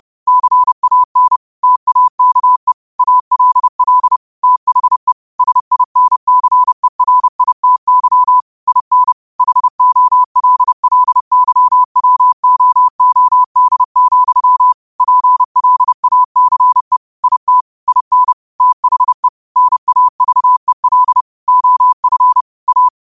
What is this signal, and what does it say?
Morse code, heard as a keyed tone, about 30 wpm, 1000 Hz. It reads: CAN TAKE ALL THE SINCERITY IN HOLLYWOOD, PLACE IT IN THE NAVEL OF A